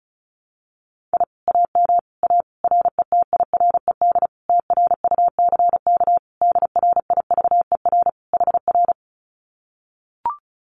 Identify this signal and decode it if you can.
Morse code, heard as a keyed tone, about 34 wpm, 700 Hz. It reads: I AM A RETIRED TRUCK DRIVER HR
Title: Common QSO Phrases Encore 35wpm